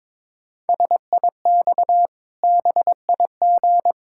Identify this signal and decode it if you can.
Morse code, heard as a keyed tone, about 21 wpm, 700 Hz. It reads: SIX BIG
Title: Pangram Challenge v2 - 22wpm